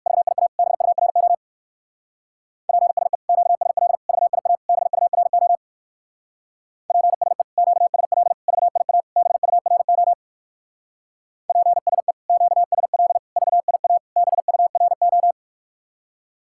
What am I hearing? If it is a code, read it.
Morse code, heard as a keyed tone, about 58 wpm, 700 Hz. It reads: VIA BURO PSE QSL VIA BURO PSE QSL VIA BURO PSE QSL VIA BURO